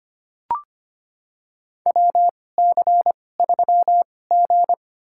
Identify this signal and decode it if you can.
Morse code, heard as a keyed tone, about 25 wpm, spaced as for 18 wpm, 700 Hz. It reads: WC3G
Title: USA Call Signs - 4 Characters - 2x Character Spacing - 25wpm